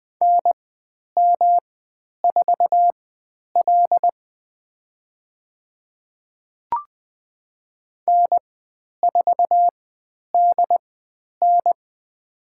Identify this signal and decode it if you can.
Morse code, heard as a keyed tone, about 20 wpm, spaced as for 10 wpm, 700 Hz. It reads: NM4L N4DN
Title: 109 Lesson 18g - Letter m - Call Signs 20wpm